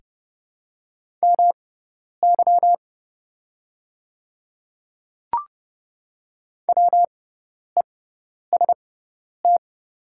Morse code, heard as a keyed tone, about 30 wpm, spaced as for 10 wpm, 700 Hz. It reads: MY WEST
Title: 6x Character Spacing - Top 500 Words - 30wpm